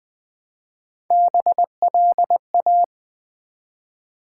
Morse code, heard as a keyed tone, about 20 wpm, 700 Hz. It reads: BLA
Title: Build A Word - 1 Letter At A Time - 5-Letter Words - 20wpm